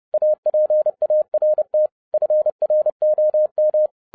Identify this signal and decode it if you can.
Morse code, heard as a keyed tone, about 30 wpm, 610 Hz. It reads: APART FROM